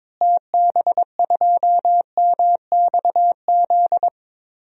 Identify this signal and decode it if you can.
Morse code, heard as a keyed tone, about 22 wpm, 700 Hz. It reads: TB2MXZ
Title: DXCC-100 Call Signs - 6 Characters - Repeated 2x - 22wpm